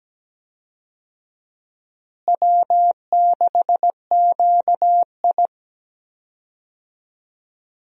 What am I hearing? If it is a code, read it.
Morse code, heard as a keyed tone, about 17 wpm, 700 Hz. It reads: W6QI
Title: USA Call Signs - 4 Characters - Repeated 3x - 17wpm